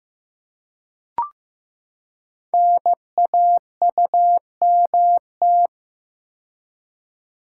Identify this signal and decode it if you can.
Morse code, heard as a keyed tone, about 15 wpm, 700 Hz. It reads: NAUMT
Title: Code Groups - 5 Letters - 15wpm